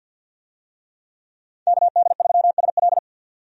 Morse code, heard as a keyed tone, about 46 wpm, 700 Hz. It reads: KD3SL
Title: USA Call Signs - 5 Characters - 50wpm